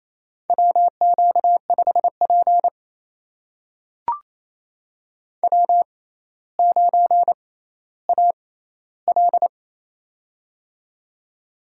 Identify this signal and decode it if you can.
Morse code, heard as a keyed tone, about 28 wpm, spaced as for 14 wpm, 700 Hz. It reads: WQ5P W9AL
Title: USA Call Signs - 4 Characters - 6x Character Spacing - 28wpm